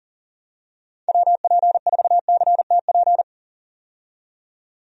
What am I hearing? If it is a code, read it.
Morse code, heard as a keyed tone, about 40 wpm, 700 Hz. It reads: WP4CTP
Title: USA Call Signs - 6 Characters - 40wpm